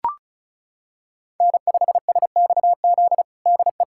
Morse code, heard as a keyed tone, about 34 wpm, 700 Hz. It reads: N5SXZ DE
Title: USA Callsign DE Callsign 35wpm